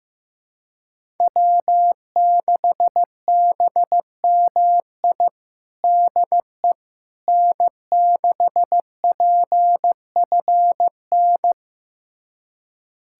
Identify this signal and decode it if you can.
Morse code, heard as a keyed tone, about 15 wpm, 700 Hz. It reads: W6BMI DE N6PFN
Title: USA Callsign DE Callsign - 5 Characters - 15wpm